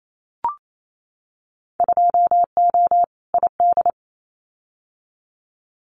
Morse code, heard as a keyed tone, about 28 wpm, 700 Hz. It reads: 2O ID
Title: Winter Field Day Exchanges 28wpm